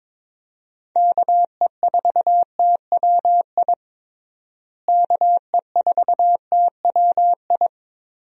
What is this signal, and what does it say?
Morse code, heard as a keyed tone, about 22 wpm, 700 Hz. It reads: KE4TWI KE4TWI
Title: USA Call Signs - 6 Characters - Repeated 3x - 22wpm